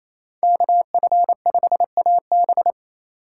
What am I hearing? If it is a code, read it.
Morse code, heard as a keyed tone, about 28 wpm, 700 Hz. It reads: KF5AB
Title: USA Call Signs - 2x Character Spacing - 28wpm